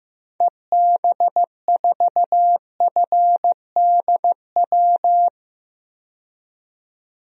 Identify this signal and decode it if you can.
Morse code, heard as a keyed tone, about 15 wpm, 700 Hz. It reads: EB4FDW
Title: DXCC-100 Call Signs - 6 Characters - 15wpm